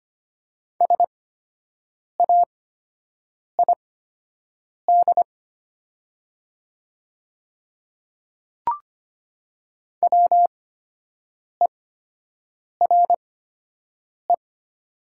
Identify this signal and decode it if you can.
Morse code, heard as a keyed tone, about 25 wpm, spaced as for 7 wpm, 700 Hz. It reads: SAID WERE
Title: 8x Character Spacing - Top 100 Words - 25wpm